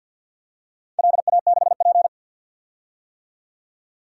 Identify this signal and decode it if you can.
Morse code, heard as a keyed tone, about 52 wpm, 700 Hz. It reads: LA6P
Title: DXCC-100 Call Signs - 4 Characters - 50wpm